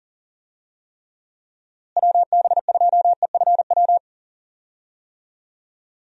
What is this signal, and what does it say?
Morse code, heard as a keyed tone, about 40 wpm, 700 Hz. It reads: WB2EFW
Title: USA Call Signs - 6 Characters - Repeated 3x - 40wpm